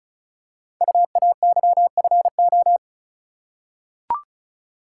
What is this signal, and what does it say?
Morse code, heard as a keyed tone, about 36 wpm, 700 Hz. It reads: UAYFO
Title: Code Groups - 5 Letters - 35wpm